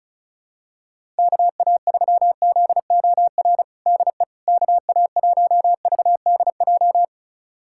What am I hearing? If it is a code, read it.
Morse code, heard as a keyed tone, about 34 wpm, 700 Hz. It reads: KA3ZOR DE KA1VDJ
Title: USA Callsign DE Callsign - 6 Characters - 35wpm